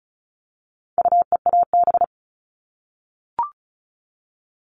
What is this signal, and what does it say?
Morse code, heard as a keyed tone, about 34 wpm, 700 Hz. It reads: UEAB
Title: All Cut Numbers - 4-Digit 35wpm